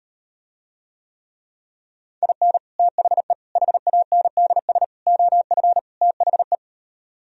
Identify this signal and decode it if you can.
Morse code, heard as a keyed tone, about 39 wpm, 700 Hz. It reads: IN THE HANDS OF THE